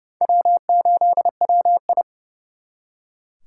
Morse code, heard as a keyed tone, about 34 wpm, 700 Hz. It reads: W8WI
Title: Speed Racing - USA 1x2 Call Signs 30wpm